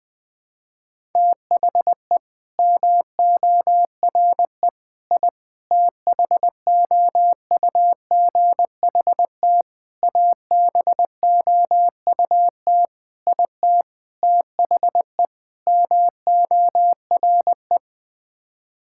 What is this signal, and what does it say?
Morse code, heard as a keyed tone, about 20 wpm, 700 Hz. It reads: THE MORE I THOUGHT ABOUT IT THE MORE
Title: Sets of 8 Words - 20wpm